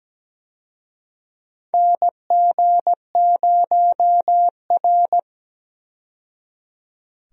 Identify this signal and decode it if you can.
Morse code, heard as a keyed tone, about 17 wpm, 700 Hz. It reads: NG0R
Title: USA Call Signs - 4 Characters - 17wpm